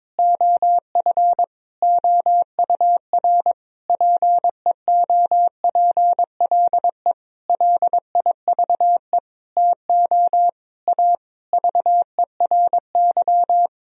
Morse code, heard as a keyed tone, about 21 wpm, 690 Hz. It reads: OF OUR PEOPLE LIVE TO A VERY
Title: Sentences from Top 1000 Words - Review Part 3 - 22wpm